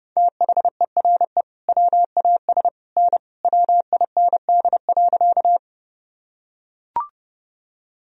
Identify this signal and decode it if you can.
Morse code, heard as a keyed tone, about 30 wpm, 700 Hz. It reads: THERE WAS N WIND.
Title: Sentences from Top 400 Words and Abbreviations 30wpm